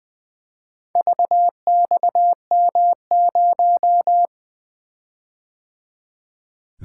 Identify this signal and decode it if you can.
Morse code, heard as a keyed tone, about 20 wpm, 700 Hz. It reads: VXM0
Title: Four Letter-Number Combinations - 20wpm